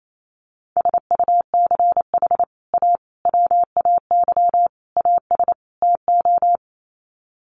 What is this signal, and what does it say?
Morse code, heard as a keyed tone, about 28 wpm, 700 Hz. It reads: SUCH A WAY AS TO
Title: Sets of 5 Words Encore2 28wpm